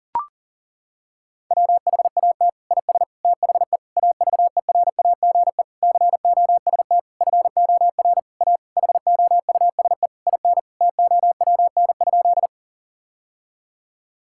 Morse code, heard as a keyed tone, about 40 wpm, 700 Hz. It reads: WHAT IS THE AVERAGE COST FOR A HOUSE IN TOWN?
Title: Sentences from Top 1000 Words - Review Part 6 - 40wpm